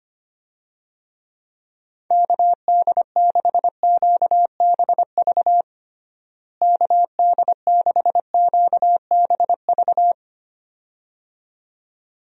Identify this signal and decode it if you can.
Morse code, heard as a keyed tone, about 25 wpm, 700 Hz. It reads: KD6QBV KD6QBV
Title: USA Call Signs - 6 Characters - Repeated 2x - 25wpm